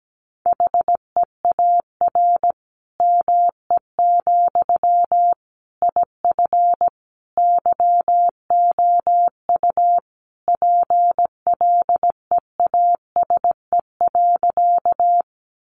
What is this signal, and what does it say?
Morse code, heard as a keyed tone, about 17 wpm, 700 Hz. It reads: HEAR ME, IF YOU PLEASE.
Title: Sentences from Top 1500 Words 17wpm